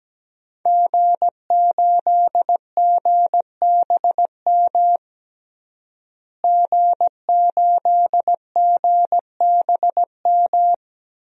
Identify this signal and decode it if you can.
Morse code, heard as a keyed tone, about 17 wpm, 700 Hz. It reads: G8GBM G8GBM
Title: DXCC-100 Call Signs - 5 Characters - Repeated 2x - 17wpm